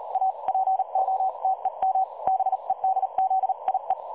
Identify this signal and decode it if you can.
Morse code, heard as a keyed tone, about 37 wpm, 760 Hz. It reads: A POTEM BEDZIE
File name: dlazaawansowanych / nagraniazpasma / websdr recording 3565.0kHz